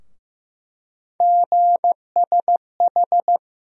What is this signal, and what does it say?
Morse code, heard as a keyed tone, about 16 wpm, 700 Hz. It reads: GSH
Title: USA Call Sign Postfixes - 3 Letters 15wpm